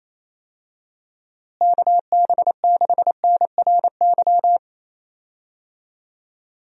Morse code, heard as a keyed tone, about 28 wpm, 700 Hz. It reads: KB6NRY
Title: USA Call Signs - 6 Characters - Repeated 3x - 28wpm